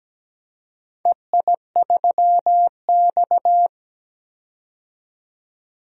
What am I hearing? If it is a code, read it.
Morse code, heard as a keyed tone, about 17 wpm, 700 Hz. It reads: EI3X
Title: DXCC-100 Call Signs - 4 Characters - 17wpm